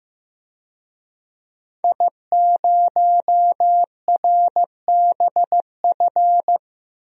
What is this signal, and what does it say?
Morse code, heard as a keyed tone, about 15 wpm, 700 Hz. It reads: I0RBF
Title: DXCC-100 Call Signs - 5 Characters - Repeated 2x - 15wpm